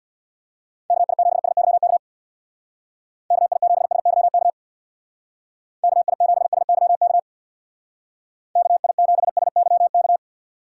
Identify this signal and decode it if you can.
Morse code, heard as a keyed tone, about 59 wpm, 700 Hz. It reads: KI7SYK KI7SYK KI7SYK KI7SYK
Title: Speed Racing - USA 2x3 Call Signs 50wpm